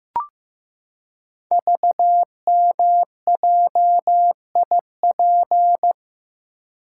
Morse code, heard as a keyed tone, about 15 wpm, 700 Hz. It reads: VMJIP
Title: Code Groups - 5 Letter:Numbers - 15wpm